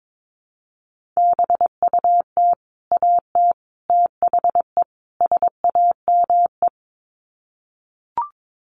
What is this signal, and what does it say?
Morse code, heard as a keyed tone, about 22 wpm, 700 Hz. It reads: BUT AT THE SAME
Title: Sets of 4 Words 22wpm